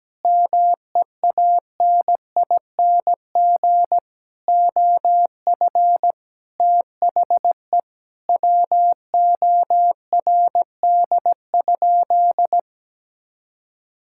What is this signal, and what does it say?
Morse code, heard as a keyed tone, about 17 wpm, 700 Hz. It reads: MEANING OF THE WORD?
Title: Sentences from Top 1000 Words - Review Part 6 - 17wpm